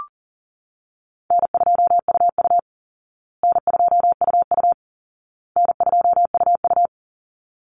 Morse code, heard as a keyed tone, about 40 wpm, 700 Hz. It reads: N2UU N2UU N2UU
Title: Easy Call Signs Repeated 3x - 40wpm